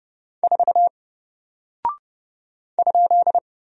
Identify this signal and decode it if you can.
Morse code, heard as a keyed tone, about 30 wpm, 700 Hz. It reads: 4 ?
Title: Single Character (Letter:Number:Punctuation:Prosign) 30wpm